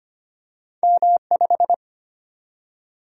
Morse code, heard as a keyed tone, about 25 wpm, 700 Hz. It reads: M5
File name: Two Letter-Number Combinations - 25wpm